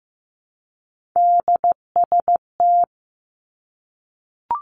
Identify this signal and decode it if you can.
Morse code, heard as a keyed tone, about 15 wpm, 700 Hz. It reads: DST
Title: Three Challenging Letter-Number Combinations 15wpm